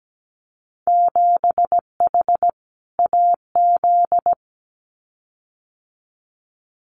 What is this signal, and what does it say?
Morse code, heard as a keyed tone, about 17 wpm, 700 Hz. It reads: 7H AZ
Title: Winter Field Day Exchanges Encore 17wpm